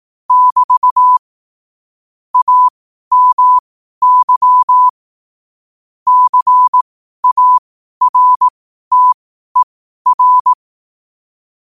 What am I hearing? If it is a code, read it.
Morse code, heard as a keyed tone, about 18 wpm, spaced as for 12 wpm, 1000 Hz. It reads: <BT> AMY CARTER
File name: qotdfiles / 12wpm